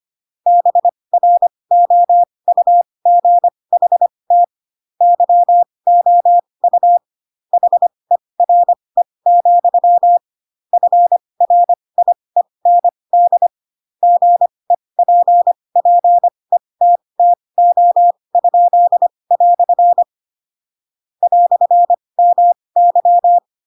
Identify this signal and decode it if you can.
Morse code, heard as a keyed tone, about 25 wpm, spaced as for 20 wpm, 700 Hz. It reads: BROUGHT YOU HERE, FRIEND GEPPETTO?" "MY
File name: Pinocchio / Pinocchio ch2 15 25 ik1hge